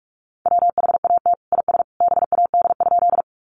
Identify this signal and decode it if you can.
Morse code, heard as a keyed tone, about 44 wpm, 700 Hz. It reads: WHAT IS BAD?
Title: Sentences from Top 700 Words 45wpm